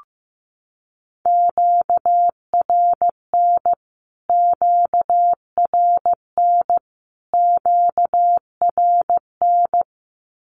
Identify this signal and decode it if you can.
Morse code, heard as a keyed tone, about 15 wpm, 700 Hz. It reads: QRN QRN QRN
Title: Q-Codes Repeated 3x 15wpm